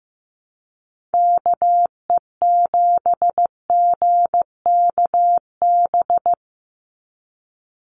Morse code, heard as a keyed tone, about 15 wpm, 700 Hz. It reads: KE7GKB
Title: USA Call Signs 15wpm